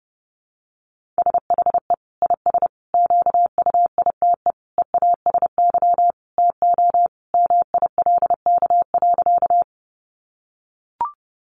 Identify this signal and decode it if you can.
Morse code, heard as a keyed tone, about 30 wpm, 700 Hz. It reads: SHE IS QUITE EASY TO MILK.
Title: Sentences from Top 1500 Words 30wpm